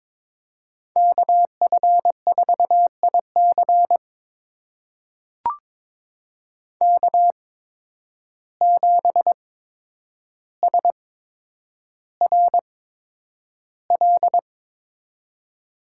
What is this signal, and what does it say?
Morse code, heard as a keyed tone, about 22 wpm, spaced as for 9 wpm, 700 Hz. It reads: KF4IC K7SRL
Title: USA Call Signs - 5 Characters - 8x Character Spacing - 22wpm